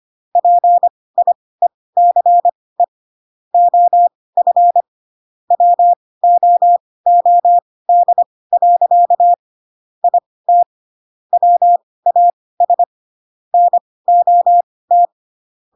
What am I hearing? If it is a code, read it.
Morse code, heard as a keyed tone, about 25 wpm, spaced as for 18 wpm, 700 Hz. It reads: PIECE OF WOOD. IT WAS NOT
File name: Pinocchio / Pinocchio ch1 12 25 ik1hge